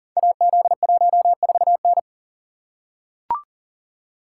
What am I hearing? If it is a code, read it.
Morse code, heard as a keyed tone, about 41 wpm, 700 Hz. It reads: AZ14N
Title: Code Groups - 5 Letter:Numbers - 40wpm